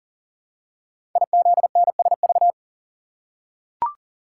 Morse code, heard as a keyed tone, about 44 wpm, 700 Hz. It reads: IZNSV
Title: Code Groups - 5 Letters - 40wpm